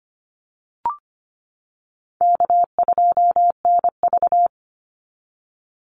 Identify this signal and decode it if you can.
Morse code, heard as a keyed tone, about 25 wpm, 700 Hz. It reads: K2NV
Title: Easy Call Signs - 25wpm